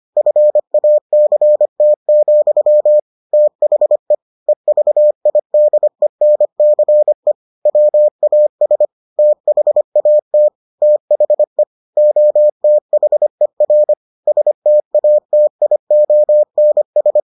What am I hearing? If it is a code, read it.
Morse code, heard as a keyed tone, about 25 wpm, 600 Hz. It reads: FACT, THE EVIDENCE WAS THAT THE OTHER STATIONS